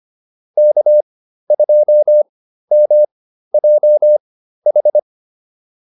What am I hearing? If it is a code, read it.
Morse code, heard as a keyed tone, about 25 wpm, spaced as for 13 wpm, 600 Hz. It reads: K2MJH